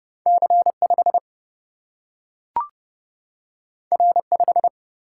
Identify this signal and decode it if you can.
Morse code, heard as a keyed tone, about 30 wpm, 700 Hz. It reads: C 5 R 5
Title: Single-Letter Single-Number 30wpm